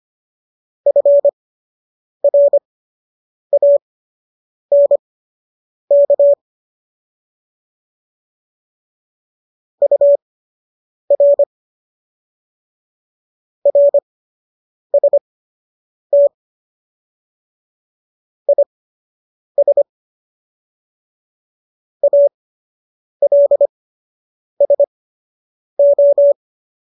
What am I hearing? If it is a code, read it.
Morse code, heard as a keyed tone, about 25 wpm, spaced as for 8 wpm, 570 Hz. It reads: FRANK UR RST IS ALSO